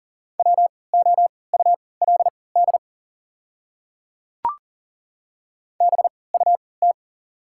Morse code, heard as a keyed tone, about 40 wpm, spaced as for 23 wpm, 690 Hz. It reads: WOULD BUT
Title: 3x Character Spacing - Top 100 Words - 40wpm